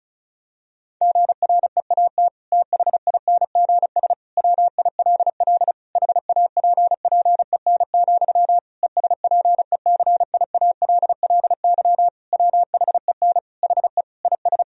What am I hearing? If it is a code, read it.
Morse code, heard as a keyed tone, about 34 wpm, 700 Hz. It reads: GREAT THINGS WILL HAPPEN, ESPECIALLY WHEN HE IS
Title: Sentences from Top 1000 Words - Review Part 4 - 35wpm